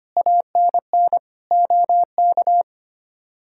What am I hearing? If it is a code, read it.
Morse code, heard as a keyed tone, about 25 wpm, 700 Hz. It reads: ANN OK
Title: K1USN SST QSO Practice - 25wpm